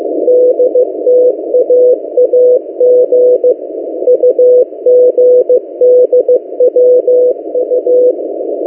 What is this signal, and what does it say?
Morse code, heard as a keyed tone, about 15 wpm, 500 Hz. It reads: DTAAG UGDWU
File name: M08a- -1355utc-8096khz